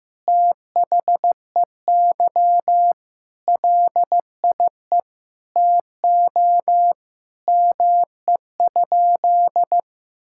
Morse code, heard as a keyed tone, about 15 wpm, 700 Hz. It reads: THEY LIE TO ME?
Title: Sentences from Top 1600 Words 15wpm